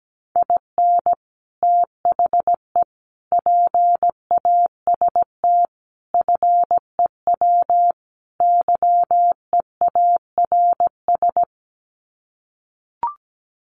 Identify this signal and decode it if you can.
Morse code, heard as a keyed tone, about 17 wpm, 700 Hz. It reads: IN THE PAST FEW YEARS
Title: Sets of 5 Words 17wpm